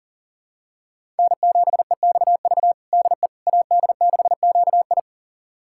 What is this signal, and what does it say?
Morse code, heard as a keyed tone, about 40 wpm, 700 Hz. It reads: N7EXV DE AD6QI
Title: USA Callsign DE Callsign - 5 Characters - 40wpm